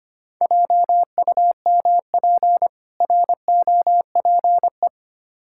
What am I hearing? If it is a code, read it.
Morse code, heard as a keyed tone, about 25 wpm, 690 Hz. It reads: JUMP ROPE
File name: practice / 25wpm / Difficult Words with Hints - 25wpm